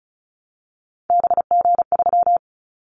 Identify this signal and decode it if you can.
Morse code, heard as a keyed tone, about 34 wpm, 700 Hz. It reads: BG3
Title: Callsign Prefixes for Top 100 DXCC - 35wpm